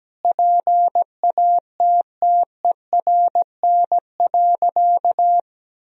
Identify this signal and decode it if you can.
Morse code, heard as a keyed tone, about 17 wpm, 700 Hz. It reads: PATTERN.
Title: Sentences from Top 1000 Words - Review Part 2 - 17wpm